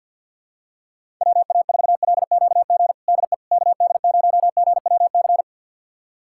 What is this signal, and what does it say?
Morse code, heard as a keyed tone, about 52 wpm, 700 Hz. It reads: WA4LQG DE KD0GWC
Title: USA Callsign DE Callsign - 6 Characters - 50wpm